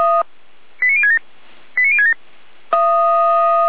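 Five-tone selective call (ZVEI), heard as quick runs of five tones, tones of about 70 ms. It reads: 89067 89067
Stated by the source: ZVEI five-tone selective call received by SDR, from the Feuerwehr Orlamünde fire brigade server.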